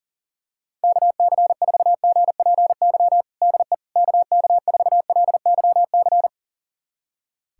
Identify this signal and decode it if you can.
Morse code, heard as a keyed tone, about 40 wpm, 700 Hz. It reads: KC4GPY DE KK4LYC
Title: USA Callsign DE Callsign - 6 Characters - 40wpm